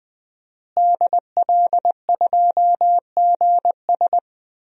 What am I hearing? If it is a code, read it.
Morse code, heard as a keyed tone, about 21 wpm, 700 Hz. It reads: DL2GS
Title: DXCC-100 Call Signs - 5 Characters - Repeated 3x - 20wpm